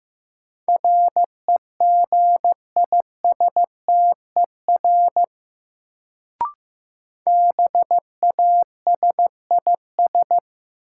Morse code, heard as a keyed tone, about 15 wpm, 700 Hz. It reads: REGISTER BASIS
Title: Sentences from Top 1200 Words - New Word Drill Alt - 15wpm